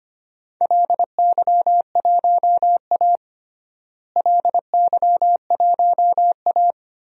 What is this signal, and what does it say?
Morse code, heard as a keyed tone, about 25 wpm, 700 Hz. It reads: LY1A LY1A
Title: DXCC-100 Call Signs - 4 Characters - Repeated 3x - 25wpm